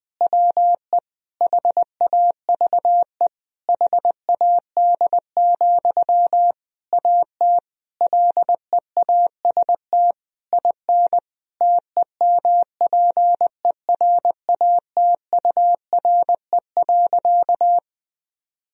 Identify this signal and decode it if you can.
Morse code, heard as a keyed tone, about 20 wpm, 700 Hz. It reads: WE HAVE HAD, AT LEAST IN TEMPERATURE.
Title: Sentences from Top 1000 Words - Review Part 4 - 20wpm